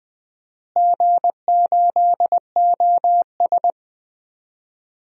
Morse code, heard as a keyed tone, about 21 wpm, 700 Hz. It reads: G8OS
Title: DXCC-100 Call Signs - 4 Characters - Repeated 2x - 20wpm